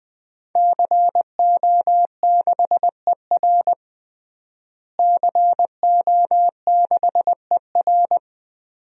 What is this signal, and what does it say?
Morse code, heard as a keyed tone, about 20 wpm, 700 Hz. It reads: CO6ER CO6ER
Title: DXCC-100 Call Signs - 5 Characters - Repeated 3x - 20wpm